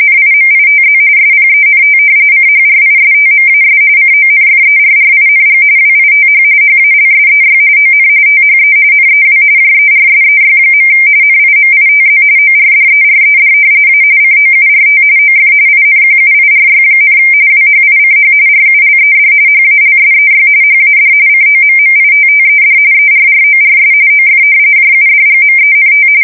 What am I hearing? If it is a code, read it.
RTTY, about 45.45 baud. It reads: NIMITZ SPENT CHRISTMAS CONFERRING WITH HALSEY ON THE NEW JERSEY. NIMITZ ORDERED A COURT OF INQUIRY, WHICH RULED HALSEY HAD COMMITTED ERRORS OF JUDGMENT
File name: fsk170 / battleshipnj